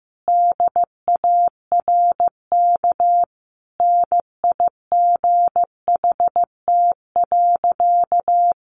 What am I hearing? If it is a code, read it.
Morse code, heard as a keyed tone, about 15 wpm, 700 Hz. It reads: DARK NIGHT.
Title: Sentences from Top 500 Words 15wpm